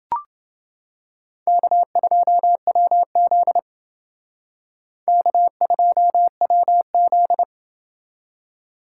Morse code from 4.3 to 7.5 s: K2WZ

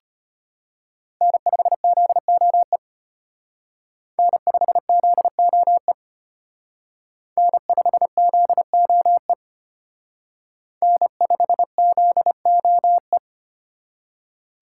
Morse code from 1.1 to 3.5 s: N5ZOE

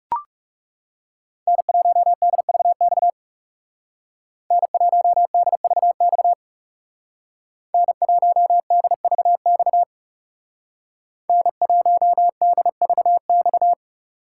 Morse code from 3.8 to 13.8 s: N1DVX N1DVX N1DVX

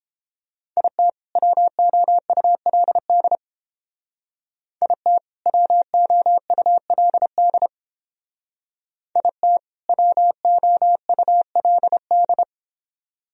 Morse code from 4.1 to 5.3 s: IT

